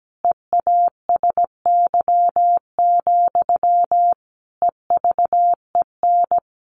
Morse code, heard as a keyed tone, about 17 wpm, 700 Hz. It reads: EASY, EVEN
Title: Sentences from Top 700 Words 17wpm